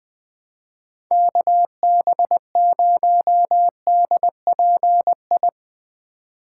Morse code, heard as a keyed tone, about 20 wpm, 700 Hz. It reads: KB0DPI
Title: Speed Racing - USA 2x3 Call Signs 17wpm